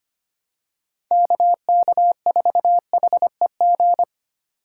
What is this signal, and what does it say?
Morse code, heard as a keyed tone, about 25 wpm, 700 Hz. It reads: KK4HEG
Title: K1USN SST QSO Practice - 25wpm